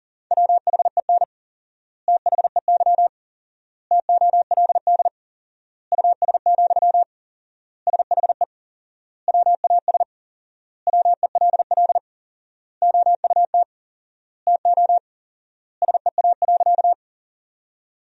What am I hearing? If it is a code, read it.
Morse code, heard as a keyed tone, about 40 wpm, 700 Hz. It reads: WHEN THEY TOLD US, SHE WAS WELL OUT TO SEA.
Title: 4x Word-Spacing - Sentences from Top 400 Words - 40wpm